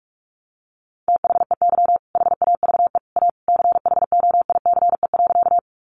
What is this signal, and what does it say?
Morse code, heard as a keyed tone, about 44 wpm, 700 Hz. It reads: THEY HAVE A CHOICE.
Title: Sentences from Top 900 Words 45wpm